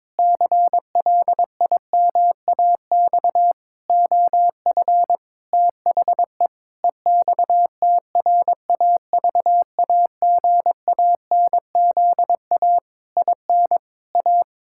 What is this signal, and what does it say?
Morse code, heard as a keyed tone, about 22 wpm, 700 Hz. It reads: CLIMAX OF THE EXTRAVAGANZA IN A
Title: Pangram Challenge v2 - 22wpm